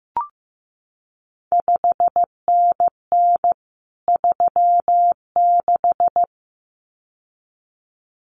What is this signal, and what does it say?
Morse code, heard as a keyed tone, about 15 wpm, 700 Hz. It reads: 5NN 36
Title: CQ WW DX Contest Exchange Encore - 15wpm